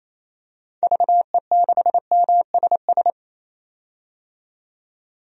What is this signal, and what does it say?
Morse code, heard as a keyed tone, about 28 wpm, 700 Hz. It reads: VE6MSS
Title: DXCC-100 Call Signs - 6 Characters - 28wpm